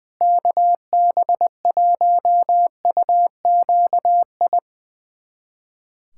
Morse code, heard as a keyed tone, about 21 wpm, 700 Hz. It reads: KB1UQI
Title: USA Callsign DE Callsign 20wpm